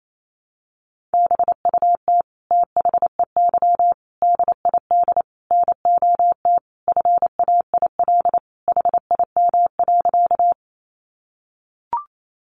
Morse code, heard as a keyed tone, about 28 wpm, 700 Hz. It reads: BUT THEY DID NOT FAIL HIM.
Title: Sentences from Top 1100 Words 28wpm